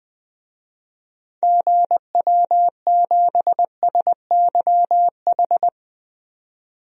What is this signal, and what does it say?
Morse code, heard as a keyed tone, about 20 wpm, 700 Hz. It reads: GW7SYH
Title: DXCC-100 Call Signs - 6 Characters - 20wpm